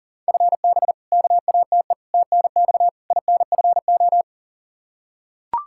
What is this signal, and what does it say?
Morse code, heard as a keyed tone, about 40 wpm, 700 Hz. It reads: FB KATE TNX INFO
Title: Practice QSO - Exchange 3 - More Info - 40wpm